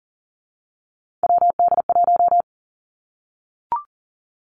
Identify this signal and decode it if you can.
Morse code, heard as a keyed tone, about 42 wpm, 700 Hz. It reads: WD1
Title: Three Challenging Letter-Number Combinations 40wpm